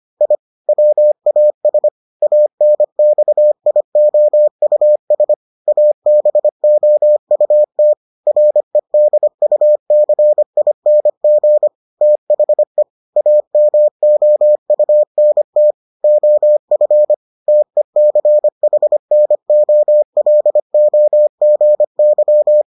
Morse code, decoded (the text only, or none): I WAS ANXIOUS ABOUT REDUCING THE AMOUNT OF TECHNOLOGY